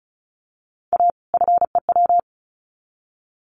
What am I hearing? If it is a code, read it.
Morse code, heard as a keyed tone, about 34 wpm, 700 Hz. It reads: A FEW
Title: Sets of 2 Words 35wpm